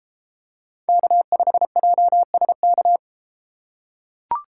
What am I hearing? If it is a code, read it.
Morse code, heard as a keyed tone, about 31 wpm, 700 Hz. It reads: K5JSK